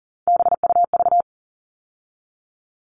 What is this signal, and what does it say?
Morse code, heard as a keyed tone, about 37 wpm, 700 Hz. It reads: BUV
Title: All Cut Numbers - 3-Digit 40wpm